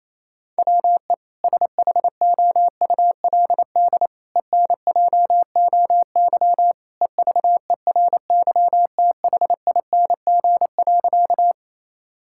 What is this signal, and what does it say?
Morse code, heard as a keyed tone, about 28 wpm, 700 Hz. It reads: WE SHOULD ENJOY EVERYTHING.
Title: Sentences from Top 1000 Words - Review Part 5 - 28wpm